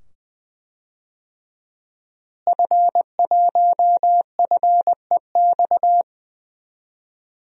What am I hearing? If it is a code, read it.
Morse code, heard as a keyed tone, about 20 wpm, 700 Hz. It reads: F1FEX
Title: DXCC-100 Call Signs - 5 Characters - Repeated 2x - 20wpm